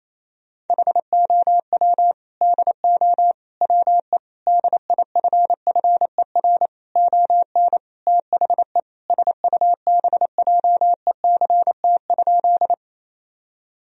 Morse code, heard as a keyed tone, about 28 wpm, 700 Hz. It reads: HOW DO WE DIFFER ON THE SUBJECT?
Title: Sentences from Top 1000 Words - Review Part 1 - 28wpm